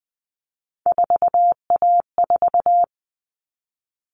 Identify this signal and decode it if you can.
Morse code, heard as a keyed tone, about 20 wpm, 700 Hz. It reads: 4A4
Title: All Cut Numbers - 3-Digit 20wpm